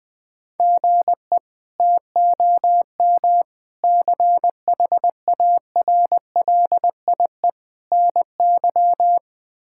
Morse code, heard as a keyed tone, about 20 wpm, 700 Hz. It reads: GE TOM CHARLIE NY
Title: K1USN SST QSO Practice - 20wpm